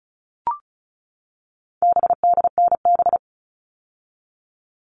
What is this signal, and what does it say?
Morse code, heard as a keyed tone, about 34 wpm, 700 Hz. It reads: BDNB
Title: All Cut Numbers - 4-Digit 35wpm